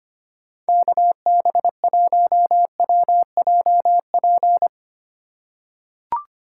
Morse code, heard as a keyed tone, about 25 wpm, 700 Hz. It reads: KB1WJP
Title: USA Call Signs - 3x Character Spacing - 25wpm